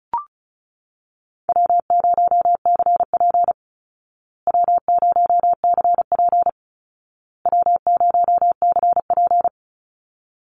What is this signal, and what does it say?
Morse code, heard as a keyed tone, about 34 wpm, 700 Hz. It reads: W0CP W0CP W0CP
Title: Easy Call Signs Repeated 3x - 35wpm